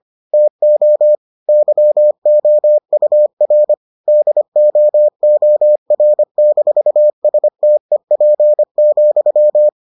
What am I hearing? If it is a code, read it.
Morse code, heard as a keyed tone, about 25 wpm, 600 Hz. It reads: TO YOUR DOOR-STEP,